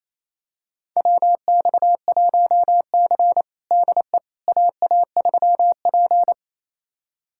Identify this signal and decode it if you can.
Morse code, heard as a keyed tone, about 28 wpm, 700 Hz. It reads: WX1C DE AA3P
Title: USA Callsign DE Callsign - 4 Characters - 28wpm